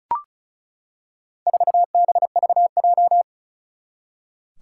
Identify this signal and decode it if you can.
Morse code, heard as a keyed tone, about 36 wpm, 700 Hz. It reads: 4BVJ
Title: Four Challenging Letter-Number Combinations 35wpm